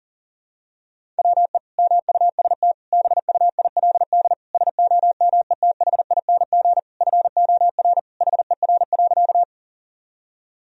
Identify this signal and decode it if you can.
Morse code, heard as a keyed tone, about 40 wpm, 700 Hz. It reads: WE MUST BUILD SOMETHING FOR HER.
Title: Sentences from Top 1000 Words - Review Part 1 - 40wpm